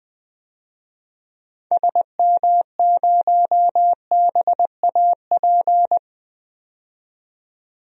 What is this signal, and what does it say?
Morse code, heard as a keyed tone, about 20 wpm, 700 Hz. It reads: SM0BAP
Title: DXCC-100 Call Signs - 6 Characters - 20wpm